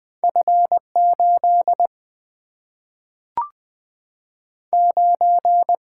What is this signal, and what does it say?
Morse code, heard as a keyed tone, about 20 wpm, 700 Hz. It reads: F 8 9
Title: Two Letter-Number Combinations - 20wpm